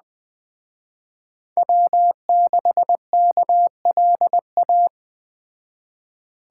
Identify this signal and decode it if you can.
Morse code, heard as a keyed tone, about 20 wpm, 700 Hz. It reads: W6KLA
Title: Speed Racing - USA 1x3 Call Signs 17wpm